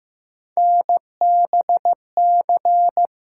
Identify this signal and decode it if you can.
Morse code, heard as a keyed tone, about 15 wpm, 700 Hz. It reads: NBC
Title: Sets of 5 Words Encore - New Word Drill - 15wpm